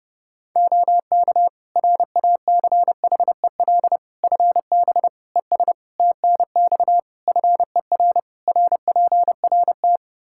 Morse code, heard as a keyed tone, about 30 wpm, 700 Hz. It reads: OK RACHEL FB ES TNX FER RPRT
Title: K1USN SST QSO Practice 30wpm